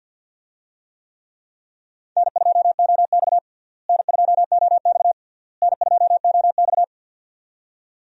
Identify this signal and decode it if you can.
Morse code, heard as a keyed tone, about 50 wpm, 700 Hz. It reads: N2OX N2OX N2OX
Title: USA Call Signs - 4 Characters - Repeated 3x - 50wpm